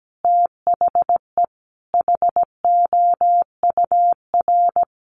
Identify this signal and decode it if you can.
Morse code, heard as a keyed tone, about 17 wpm, 700 Hz. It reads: THE HOUR
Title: Sets of 5 Words Encore2 17wpm